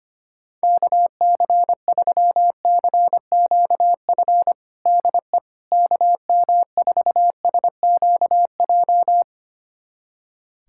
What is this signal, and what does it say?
Morse code, heard as a keyed tone, about 25 wpm, 700 Hz. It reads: KC3CQF DE KM4SQJ
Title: USA Callsign DE Callsign - 6 Characters - 25wpm